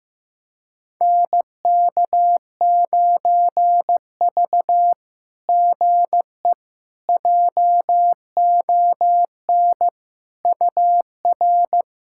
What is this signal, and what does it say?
Morse code, heard as a keyed tone, about 15 wpm, 700 Hz. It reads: NK9V GE JON UR
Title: SOTA QSO Practice 15wpm